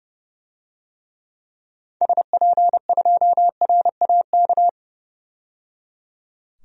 Morse code, heard as a keyed tone, about 30 wpm, 700 Hz. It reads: SP2RAK
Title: DXCC-100 Call Signs - 6 Characters - 30wpm